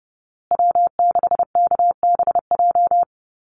Morse code, heard as a keyed tone, about 30 wpm, 700 Hz. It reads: W6KBJ
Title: USA Call Signs Repeated 3x 30wpm